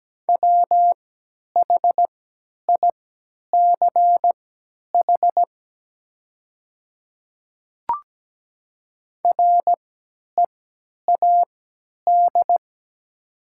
Morse code, heard as a keyed tone, about 17 wpm, spaced as for 10 wpm, 700 Hz. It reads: WHICH READ
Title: 3x Character Spacing - Top 500 Words - 17wpm